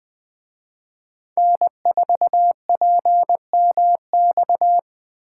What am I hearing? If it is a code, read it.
Morse code, heard as a keyed tone, about 20 wpm, 700 Hz. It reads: N4PMX
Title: Speed Racing - USA Call Signs 17wpm